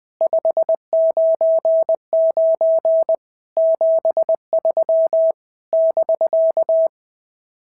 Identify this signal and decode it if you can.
Morse code, heard as a keyed tone, about 20 wpm, 650 Hz. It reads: 599 73 <BK>
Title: SOTA QSO Practice 20wpm